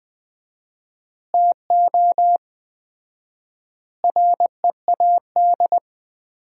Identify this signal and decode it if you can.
Morse code, heard as a keyed tone, about 20 wpm, 700 Hz. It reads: TO READ
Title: 4x Word-Spacing - Sentences from Top 500 Words - 20wpm